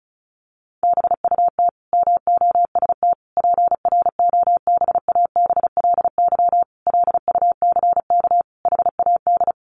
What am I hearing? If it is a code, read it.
Morse code, heard as a keyed tone, about 34 wpm, 700 Hz. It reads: BUT MOST PROBABLY LUCK HAD
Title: Sentences from Top 1500 Words 35wpm